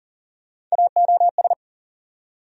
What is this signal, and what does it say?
Morse code, heard as a keyed tone, about 40 wpm, 700 Hz. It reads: AOS
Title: USA Call Sign Postfixes - 3 Letters 40wpm